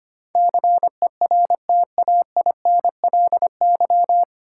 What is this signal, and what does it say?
Morse code, heard as a keyed tone, about 25 wpm, 700 Hz. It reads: CERTAINLY
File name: Nine Letter Words - 25wpm